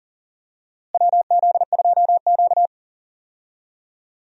Morse code, heard as a keyed tone, about 40 wpm, 700 Hz. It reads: WZ2Q
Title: USA Call Signs - 4 Characters - Repeated 2x - 40wpm